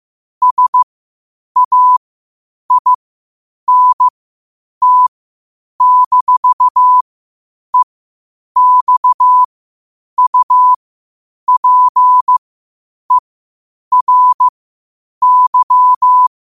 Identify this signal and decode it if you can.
Morse code, heard as a keyed tone, about 15 wpm, spaced as for 8 wpm, 1000 Hz. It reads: SAINT-EXUPERY
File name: qotdfiles / 8wpm